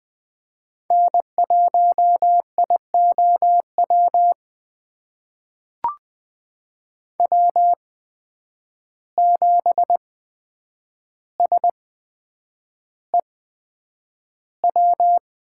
Morse code, heard as a keyed tone, about 20 wpm, spaced as for 9 wpm, 700 Hz. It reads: N1IOW W7SEW
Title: USA Call Signs - 5 Characters - 8x Character Spacing - 20wpm